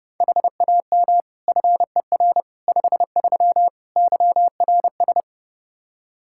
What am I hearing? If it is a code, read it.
Morse code, heard as a keyed tone, about 30 wpm, 700 Hz. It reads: HAM FER 53 YRS
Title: Practice QSO - Exchange 3 - More Info - 30wpm